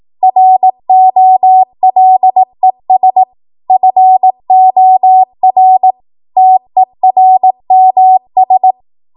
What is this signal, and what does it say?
Morse code, heard as a keyed tone, about 18 wpm, 750 Hz. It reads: ROLES FOR TERMS